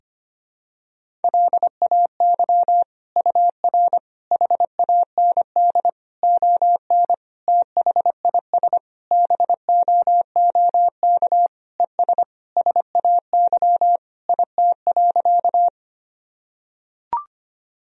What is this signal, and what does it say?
Morse code, heard as a keyed tone, about 25 wpm, 700 Hz. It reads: LAY UR HAND ON THIS BOOK ES SAY IT.
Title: Sentences from Top 400 Words and Abbreviations 25wpm